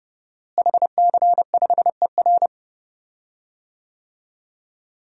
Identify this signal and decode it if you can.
Morse code, heard as a keyed tone, about 30 wpm, 700 Hz. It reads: HC5ER
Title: DXCC-100 Call Signs - 5 Characters - Repeated 2x - 30wpm